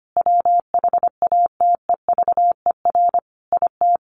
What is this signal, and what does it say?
Morse code, heard as a keyed tone, about 25 wpm, 700 Hz. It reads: WHATEVER IT
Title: Sentences from Top 1100 Words 25wpm